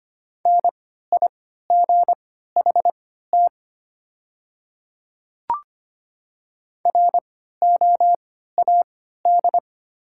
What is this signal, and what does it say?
Morse code, heard as a keyed tone, about 25 wpm, spaced as for 14 wpm, 700 Hz. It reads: NIGHT ROAD
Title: 3x Character Spacing - Top 500 Words - 25wpm